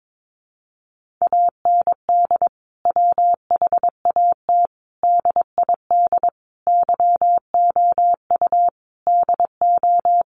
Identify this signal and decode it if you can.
Morse code, heard as a keyed tone, about 22 wpm, 700 Hz. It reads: AND WHAT DID YOU DO
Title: Sentences from Top 100 Words 22wpm